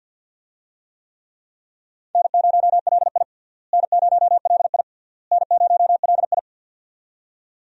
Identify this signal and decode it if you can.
Morse code, heard as a keyed tone, about 50 wpm, 700 Hz. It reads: N0LI N0LI N0LI
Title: USA Call Signs - 4 Characters - Repeated 3x - 50wpm